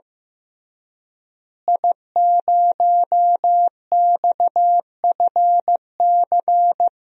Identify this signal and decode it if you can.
Morse code, heard as a keyed tone, about 15 wpm, 700 Hz. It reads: I0XFC
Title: DXCC-100 Call Signs - 5 Characters - Repeated 3x - 15wpm